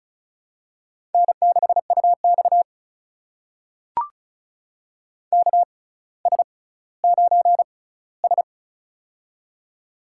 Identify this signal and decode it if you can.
Morse code, heard as a keyed tone, about 36 wpm, spaced as for 18 wpm, 700 Hz. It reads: N6UX KS9S
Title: USA Call Signs - 4 Characters - 6x Character Spacing - 35wpm